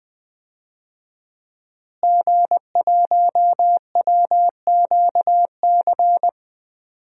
Morse code, heard as a keyed tone, about 21 wpm, 700 Hz. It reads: G1WQC
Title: DXCC-100 Call Signs - 5 Characters - Repeated 2x - 20wpm